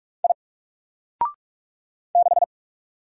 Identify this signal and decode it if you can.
Morse code, heard as a keyed tone, about 44 wpm, 700 Hz. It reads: I 6
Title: Single Character (Letter:Number:Punctuation:Prosign) 45wpm